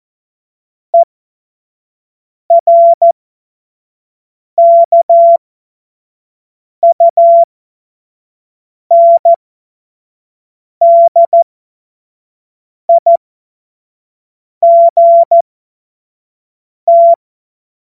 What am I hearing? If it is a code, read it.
Morse code, heard as a keyed tone, about 14 wpm, spaced as for 5 wpm, 680 Hz. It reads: ERKUNDIGT
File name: Morsekurs / Witze / Witz 04. Juli 2023 25 F70